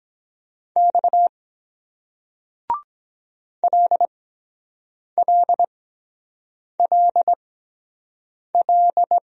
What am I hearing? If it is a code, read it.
Morse code, heard as a keyed tone, about 23 wpm, spaced as for 7 wpm, 700 Hz. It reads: X LLLL